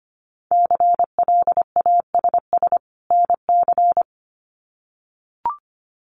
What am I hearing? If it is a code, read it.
Morse code, heard as a keyed tone, about 25 wpm, 700 Hz. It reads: CLASS NC
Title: Winter Field Day Exchanges 25wpm